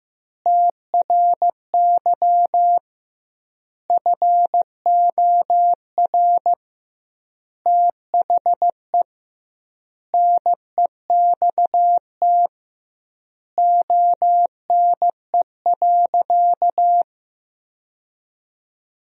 Morse code, heard as a keyed tone, about 15 wpm, 700 Hz. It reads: TRY FOR THE NEXT ONE.
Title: 2x Word-Spacing - Sentences from Top 400 Words - 15wpm